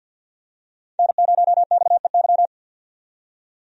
Morse code, heard as a keyed tone, about 52 wpm, 700 Hz. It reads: N0XEY
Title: USA Call Signs - 5 Characters - 50wpm